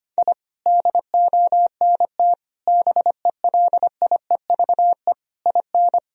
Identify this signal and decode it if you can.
Morse code, heard as a keyed tone, about 25 wpm, 700 Hz. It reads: I DONT BELIEVE IN
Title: Sentences from Top 1000 Words - Review Part 2 - 25wpm